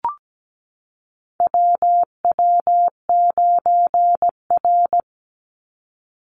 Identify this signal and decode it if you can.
Morse code, heard as a keyed tone, about 17 wpm, 700 Hz. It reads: WW9R
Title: Call Signs Repeated 3x - 17wpm